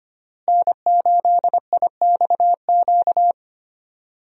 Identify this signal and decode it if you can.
Morse code, heard as a keyed tone, about 25 wpm, 700 Hz. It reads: N8IXQ
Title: USA Call Signs - 5 Characters - Repeated 2x - 25wpm